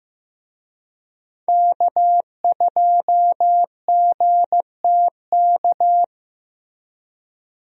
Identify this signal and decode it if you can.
Morse code, heard as a keyed tone, about 15 wpm, 700 Hz. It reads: K2GTK
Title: USA Call Signs - 5 Characters - Repeated 3x - 15wpm